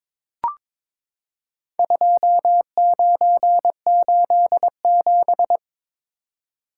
Morse code, heard as a keyed tone, about 22 wpm, 700 Hz. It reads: 2987
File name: practice / 22wpm / Four Letter-Number Combinations - 22wpm